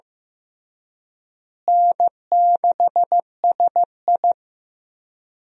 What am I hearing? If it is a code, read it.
Morse code, heard as a keyed tone, about 15 wpm, 700 Hz. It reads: N6SI
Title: USA Call Signs - 4 Characters - Repeated 3x - 15wpm